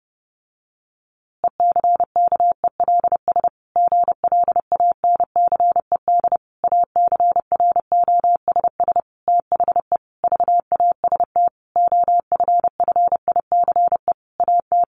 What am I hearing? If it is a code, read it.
Morse code, heard as a keyed tone, about 30 wpm, 700 Hz. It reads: ECKELS GLANCED ACROSS THE VAST OFFICE AT
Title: A Sound Of Thunder 30wpm